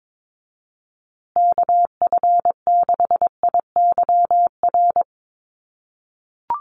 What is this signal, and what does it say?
Morse code, heard as a keyed tone, about 22 wpm, 700 Hz. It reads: KF6IYR
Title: USA Call Signs Repeated 3x 22wpm